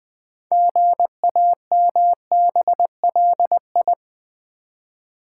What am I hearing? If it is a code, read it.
Morse code, heard as a keyed tone, about 20 wpm, 700 Hz. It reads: GAMBLI
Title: Build A Word - 3 Letters At A Time - 8-Letter Words - 20wpm